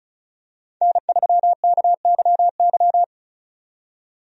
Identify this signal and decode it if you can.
Morse code, heard as a keyed tone, about 36 wpm, 700 Hz. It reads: N3KYY
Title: USA Call Signs - 5 Characters - 35wpm